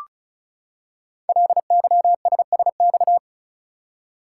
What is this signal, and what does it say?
Morse code, heard as a keyed tone, about 36 wpm, 690 Hz. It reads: LYSSX
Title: Code Groups - 5 Letters - 35wpm